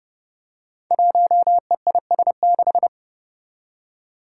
Morse code, heard as a keyed tone, about 30 wpm, 700 Hz. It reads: 1EIS6
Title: Five Challenging Letter-Number Combinations 30wpm